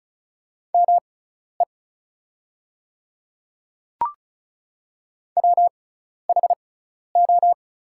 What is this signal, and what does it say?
Morse code, heard as a keyed tone, about 36 wpm, spaced as for 12 wpm, 690 Hz. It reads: ME WHO
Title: 6x Character Spacing - Top 200 Words - 35wpm